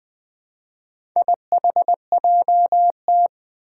Morse code, heard as a keyed tone, about 20 wpm, 700 Hz. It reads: IHJT
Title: Four Challenging Letter-Number Combinations 20wpm